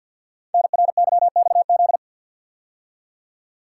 Morse code, heard as a keyed tone, about 46 wpm, 700 Hz. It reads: NRYXZ
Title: Code Groups - 5 Letters - 50wpm